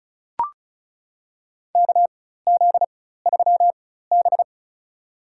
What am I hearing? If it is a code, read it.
Morse code, heard as a keyed tone, about 36 wpm, spaced as for 16 wpm, 700 Hz. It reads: KZ3B
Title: USA Call Signs - 4 Characters - 4x Character Spacing - 35wpm